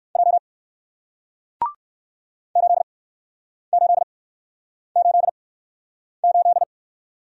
Morse code, heard as a keyed tone, about 56 wpm, spaced as for 9 wpm, 700 Hz. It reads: J 8888
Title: Speed-Racing Single Letter-Number 45wpm